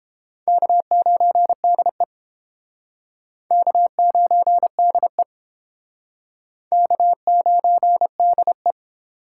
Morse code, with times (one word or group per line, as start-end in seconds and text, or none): 0.5-2.0 K9DE
3.5-5.2 K9DE
6.7-8.7 K9DE